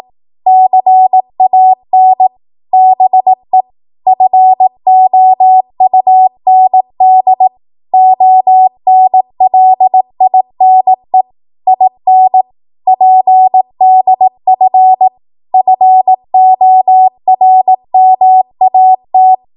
Morse code, heard as a keyed tone, about 18 wpm, 750 Hz. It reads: CAN BE FOUND ONLINE IN PDF FORMAT
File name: Morse / 18WPM